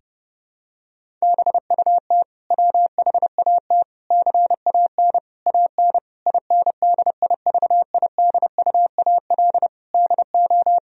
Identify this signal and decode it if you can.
Morse code, heard as a keyed tone, about 30 wpm, 700 Hz. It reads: BUT WHAT CAN AN INDIVIDUAL DO
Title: Sentences from Top 1000 Words - Review Part 4 - 30wpm